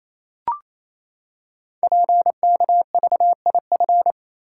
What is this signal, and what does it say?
Morse code, heard as a keyed tone, about 27 wpm, 700 Hz. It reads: PKVIF
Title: Code Groups - 5 Letters - 28wpm